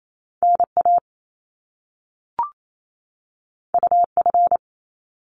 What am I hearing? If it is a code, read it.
Morse code, heard as a keyed tone, about 28 wpm, 700 Hz. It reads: NA UF
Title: Top Two Letter Combinations in Words 28wpm